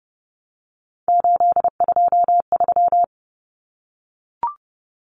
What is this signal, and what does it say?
Morse code, heard as a keyed tone, about 30 wpm, 700 Hz. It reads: 823
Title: Common Cut Numbers - 3-Digit 30wpm